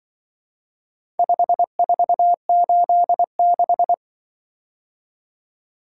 Morse code, heard as a keyed tone, about 24 wpm, 700 Hz. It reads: 5486
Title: Speed-Racing - Four Letter-Number Combinations - 20wpm